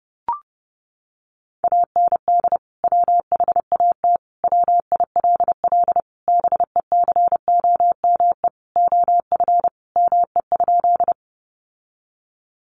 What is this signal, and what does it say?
Morse code, heard as a keyed tone, about 30 wpm, 700 Hz. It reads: AND WHAT WILL BECOME OF ME?
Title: Sentences from Top 600 Words 30wpm